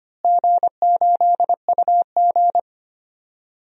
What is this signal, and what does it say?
Morse code, heard as a keyed tone, about 25 wpm, 700 Hz. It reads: G8UG
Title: DXCC-100 Call Signs - 4 Characters - Repeated 2x - 25wpm